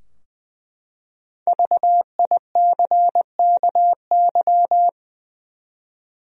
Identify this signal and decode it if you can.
Morse code, heard as a keyed tone, about 20 wpm, 700 Hz. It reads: VICKY
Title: USA Common Names - Max 5 Letters - 20wpm